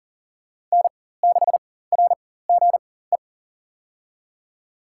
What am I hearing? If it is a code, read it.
Morse code, heard as a keyed tone, about 37 wpm, spaced as for 18 wpm, 700 Hz. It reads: N6RGE
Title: USA Call Signs - 4x Character Spacing - 40wpm